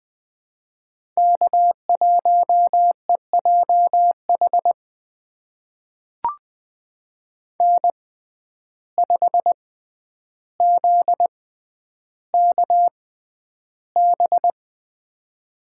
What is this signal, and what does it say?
Morse code, heard as a keyed tone, about 20 wpm, spaced as for 10 wpm, 700 Hz. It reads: K1EJH N5ZKB
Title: USA Call Signs - 5 Characters - 6x Character Spacing - 20wpm